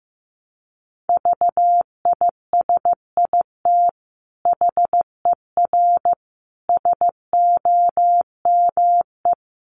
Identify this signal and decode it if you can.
Morse code, heard as a keyed tone, about 15 wpm, 700 Hz. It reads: VISIT HER SOME
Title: Sentences from Top 900 Words 15wpm